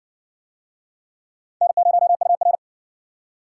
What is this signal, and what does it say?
Morse code, heard as a keyed tone, about 59 wpm, 700 Hz. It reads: N0UR